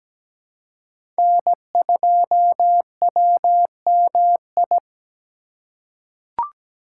Morse code, heard as a keyed tone, about 17 wpm, 700 Hz. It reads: N2WMI
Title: USA Call Signs - 4x Character Spacing - 17wpm